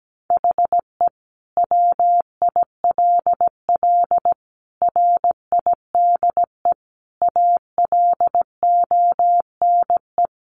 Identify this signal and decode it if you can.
Morse code, heard as a keyed tone, about 17 wpm, 700 Hz. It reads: HE WILL RIDE ALONE
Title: Sentences from Top 1400 Words 17wpm